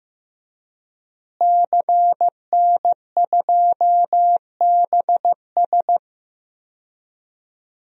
Morse code, heard as a keyed tone, about 15 wpm, 700 Hz. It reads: CN2BS
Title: DXCC-100 Call Signs - 5 Characters - Repeated 2x - 15wpm